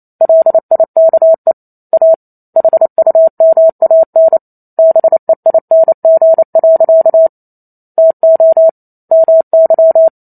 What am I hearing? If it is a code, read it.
Morse code, heard as a keyed tone, about 28 wpm, 650 Hz. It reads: LIKE A HUMAN BEING. TO MY